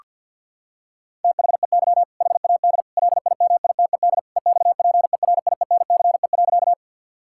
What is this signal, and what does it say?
Morse code, heard as a keyed tone, about 50 wpm, 700 Hz. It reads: THEY HAD LIMITED EXPERIENCE.
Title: Sentences from Top 1000 Words - Review Part 6 - 50wpm